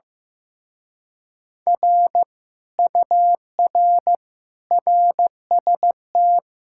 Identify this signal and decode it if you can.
Morse code, heard as a keyed tone, about 15 wpm, 700 Hz. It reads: R UR RST
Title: Speed-Racing Common QSO Phrases Encore - 15wpm